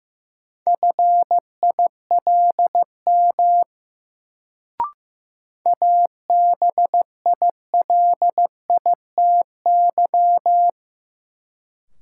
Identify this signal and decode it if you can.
Morse code, heard as a keyed tone, about 15 wpm, 700 Hz. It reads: FILM ABILITY
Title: Sentences from Top 1000 Words - Review Part 3 - New Word Drill - 15wpm